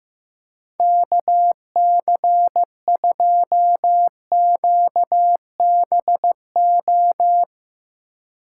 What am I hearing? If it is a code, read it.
Morse code, heard as a keyed tone, about 15 wpm, 700 Hz. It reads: KC2QBO
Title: USA Call Signs - 6 Characters - 3x Character Spacing - 15wpm